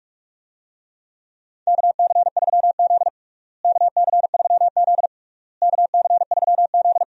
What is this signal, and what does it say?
Morse code, heard as a keyed tone, about 45 wpm, 700 Hz. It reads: KC3Z KC3Z KC3Z
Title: USA Call Signs - 4 Characters - Repeated 3x - 45wpm